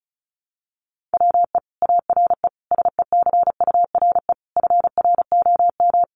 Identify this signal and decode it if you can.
Morse code, heard as a keyed tone, about 34 wpm, 700 Hz. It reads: WE ARE SECURE FROM
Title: Sentences from Top 1200 Words 35wpm